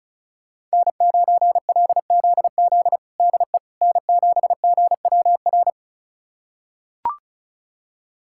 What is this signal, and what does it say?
Morse code, heard as a keyed tone, about 34 wpm, 700 Hz. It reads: N9LZZ DE N7GWR
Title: USA Callsign DE Callsign - 5 Characters - 35wpm